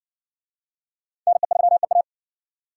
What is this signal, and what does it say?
Morse code, heard as a keyed tone, about 60 wpm, 700 Hz. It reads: NE3EA